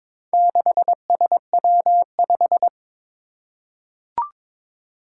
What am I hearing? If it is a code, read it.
Morse code, heard as a keyed tone, about 22 wpm, 700 Hz. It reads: 6SW5
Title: Four Challenging Letter-Number Combinations 22wpm